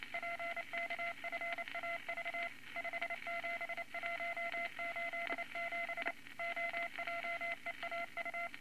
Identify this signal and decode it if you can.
Morse code, heard as a keyed tone, about 28 wpm, 2050 Hz, with noise. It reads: PKFAV 57187 OJEAU
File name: Morse Code Records / 22 A LCM 172 Twenty Words Per Minute